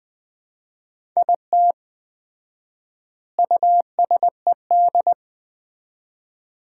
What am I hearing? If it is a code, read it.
Morse code, heard as a keyed tone, about 19 wpm, 700 Hz. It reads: IT USED
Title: 4x Word-Spacing - Sets of 4 Words - 20wpm